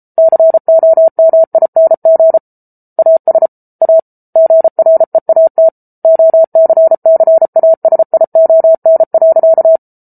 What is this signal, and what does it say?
Morse code, heard as a keyed tone, about 34 wpm, 650 Hz. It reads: COMING AS A GREAT OCCASION.